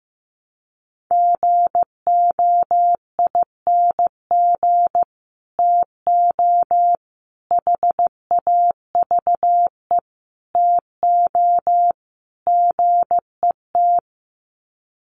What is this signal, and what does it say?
Morse code, heard as a keyed tone, about 15 wpm, 700 Hz. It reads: GOING TO HAVE TO GET
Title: Sets of 5 Words Encore2 15wpm